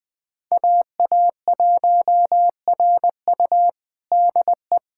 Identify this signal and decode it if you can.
Morse code, heard as a keyed tone, about 20 wpm, 700 Hz. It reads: AA1RU DE
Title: USA Callsign DE Callsign - 5 Characters - 20wpm